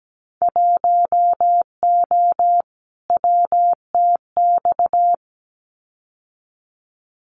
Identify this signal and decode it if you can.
Morse code, heard as a keyed tone, about 17 wpm, 700 Hz. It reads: 1O WTX
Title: Winter Field Day Exchanges Encore 17wpm